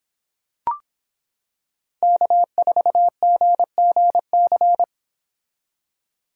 Morse code, heard as a keyed tone, about 25 wpm, 700 Hz. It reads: K4GGC